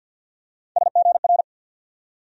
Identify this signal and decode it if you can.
Morse code, heard as a keyed tone, about 49 wpm, 700 Hz. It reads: SGR
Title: USA Call Sign Postfixes - 3 Letters 50wpm